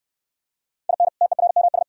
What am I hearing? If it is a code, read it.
Morse code, heard as a keyed tone, about 68 wpm, 700 Hz. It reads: IN TERMS